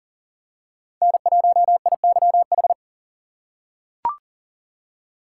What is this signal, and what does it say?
Morse code, heard as a keyed tone, about 40 wpm, 700 Hz. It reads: N1IYH
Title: USA Call Signs - 3x Character Spacing - 40wpm